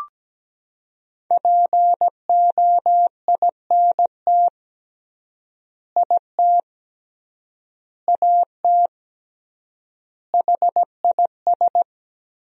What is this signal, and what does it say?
Morse code, heard as a keyed tone, about 17 wpm, 700 Hz. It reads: POINT IT AT HIS
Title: 3x Word-Spacing - Sentences from Top 300 Words - 17wpm